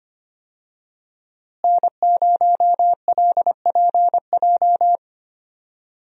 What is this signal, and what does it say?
Morse code, heard as a keyed tone, about 25 wpm, 700 Hz. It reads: N0LPJ
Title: USA Call Signs - 5 Characters - Repeated 2x - 25wpm